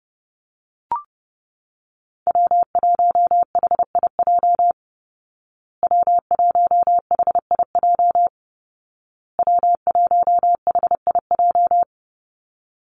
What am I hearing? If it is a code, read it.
Morse code, heard as a keyed tone, about 30 wpm, 700 Hz. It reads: W1HIJ W1HIJ W1HIJ
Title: Call Signs Repeated 2x - 30wpm